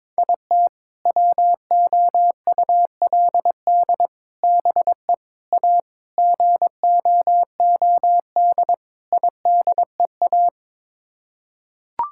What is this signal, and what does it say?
Morse code, heard as a keyed tone, about 22 wpm, 700 Hz. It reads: IT WOULD BE A GOOD IDEA
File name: practice / 22wpm / Sets of 6 Words - 22wpm